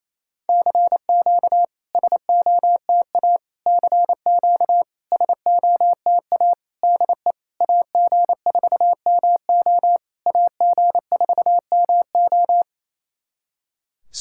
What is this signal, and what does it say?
Morse code, heard as a keyed tone, about 27 wpm, 700 Hz. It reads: CQ SOTA CQ SOTA DE AG4MO AG4MO
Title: SOTA QSO Practice 28wpm